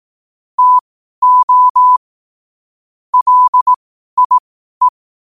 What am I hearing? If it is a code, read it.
Morse code, heard as a keyed tone, about 18 wpm, spaced as for 12 wpm, 1000 Hz. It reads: TO LIE
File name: qotdfiles / 12wpm